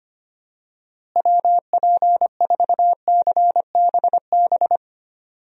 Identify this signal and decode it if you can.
Morse code, heard as a keyed tone, about 25 wpm, 700 Hz. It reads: WP4CBB
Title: USA Call Signs - 6 Characters - Repeated 3x - 25wpm